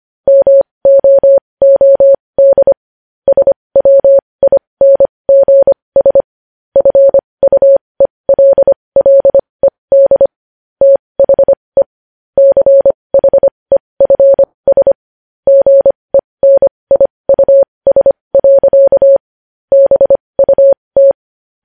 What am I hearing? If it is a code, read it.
Morse code, heard as a keyed tone, about 25 wpm, spaced as for 20 wpm, 550 Hz. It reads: MOOD SWINGS FUELLED THE CHEFS GENIUS. BUT
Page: News headlines in Morse code at 20 WPM